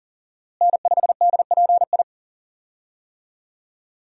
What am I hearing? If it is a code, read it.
Morse code, heard as a keyed tone, about 40 wpm, 700 Hz. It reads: N5DPI
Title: USA Call Signs - 5 Characters - Repeated 3x - 40wpm